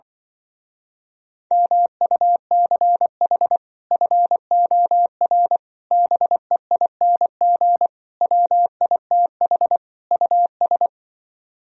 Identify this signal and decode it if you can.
Morse code, heard as a keyed tone, about 24 wpm, 700 Hz. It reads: MUCH FOR BEING WITH US
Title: Speed-Racing - Sets of 5 Words 20wpm